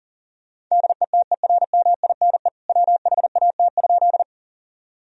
Morse code, heard as a keyed tone, about 40 wpm, 700 Hz. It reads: DETERMINE WHAT?
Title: Sentences from Top 1000 Words - Review Part 4 - 40wpm